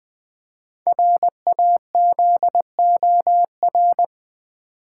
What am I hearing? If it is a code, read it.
Morse code, heard as a keyed tone, about 20 wpm, 700 Hz. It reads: RAZOR
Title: Easy Words with Hints - 20wpm